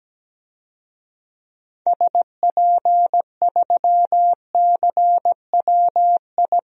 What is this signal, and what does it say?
Morse code, heard as a keyed tone, about 17 wpm, 700 Hz. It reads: SP3CWI
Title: DXCC-100 Call Signs - 6 Characters - 17wpm